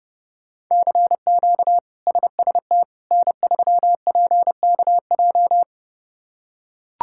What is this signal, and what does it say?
Morse code, heard as a keyed tone, about 32 wpm, 700 Hz. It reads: CQ SST N3PKJ
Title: K1USN SST QSO Practice - 30wpm